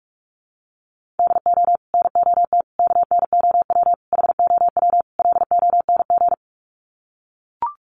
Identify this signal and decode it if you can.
Morse code, heard as a keyed tone, about 45 wpm, 700 Hz. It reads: DO NOT KNOW HOW LONG
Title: Sets of 5 Words Encore2 45wpm